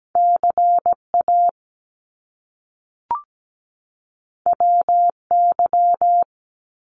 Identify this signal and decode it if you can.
Morse code, heard as a keyed tone, about 17 wpm, 700 Hz. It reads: CA WY
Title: US States 17wpm